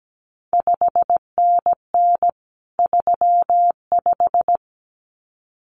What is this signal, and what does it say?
Morse code, heard as a keyed tone, about 17 wpm, 700 Hz. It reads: 5NN 35
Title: CQ WW DX Contest Exchange Encore - 17wpm